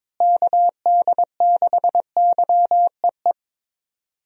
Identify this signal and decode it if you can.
Morse code, heard as a keyed tone, about 22 wpm, 700 Hz. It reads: KD6YEE
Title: USA Call Signs - 6 Characters - Repeated 3x - 22wpm